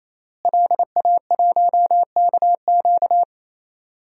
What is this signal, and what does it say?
Morse code, heard as a keyed tone, about 28 wpm, 700 Hz. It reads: LA1KQ
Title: DXCC-100 Call Signs - 5 Characters - Repeated 3x - 28wpm